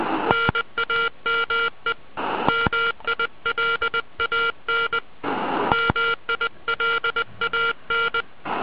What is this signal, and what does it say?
Morse code, heard as a keyed tone, about 20 wpm, 1390 Hz. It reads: NAME MILAN MILAN